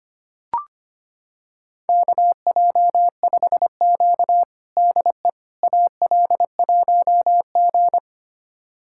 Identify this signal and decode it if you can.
Morse code, heard as a keyed tone, about 25 wpm, 700 Hz. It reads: KJ5Q DE AL1G
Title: USA Callsign DE Callsign - 4 Characters - 25wpm